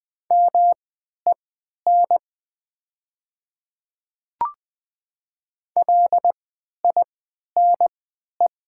Morse code, heard as a keyed tone, about 20 wpm, spaced as for 11 wpm, 700 Hz. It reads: MEN LINE
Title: 3x Character Spacing - Top 200 Words - 20wpm